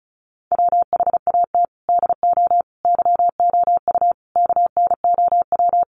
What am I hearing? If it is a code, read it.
Morse code, heard as a keyed tone, about 34 wpm, 700 Hz. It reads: WHAT DO YOU KNOW
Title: Sets of 5 Words Encore 35wpm